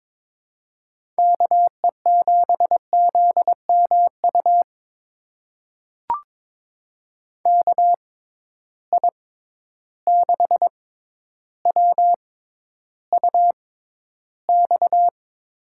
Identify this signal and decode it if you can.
Morse code, heard as a keyed tone, about 22 wpm, spaced as for 11 wpm, 700 Hz. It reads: KE7ZMU KI6WUX
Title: USA Call Signs - 6 Characters - 6x Character Spacing - 22wpm